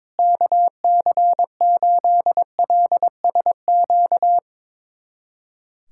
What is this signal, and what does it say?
Morse code, heard as a keyed tone, about 22 wpm, 700 Hz. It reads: KC8LSQ
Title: USA Callsign DE Callsign - 6 Characters - 22wpm